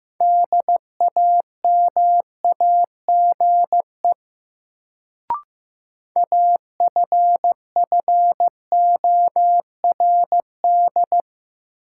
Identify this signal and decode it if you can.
Morse code, heard as a keyed tone, about 16 wpm, 700 Hz. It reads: DAMAGE AFFORD
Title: Sentences from Top 1100 Words - New Word Drill - 15wpm